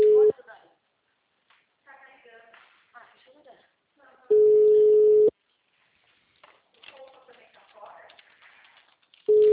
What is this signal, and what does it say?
Ringback tone (Europe, 425 Hz), with speech.